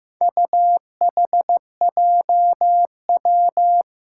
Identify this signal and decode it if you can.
Morse code, heard as a keyed tone, about 15 wpm, 700 Hz. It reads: UHJW
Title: Four Challenging Letter-Number Combinations 15wpm